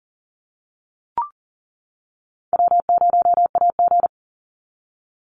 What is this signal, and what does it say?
Morse code, heard as a keyed tone, about 38 wpm, 700 Hz. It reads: W0AG
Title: Easy Call Signs - 40wpm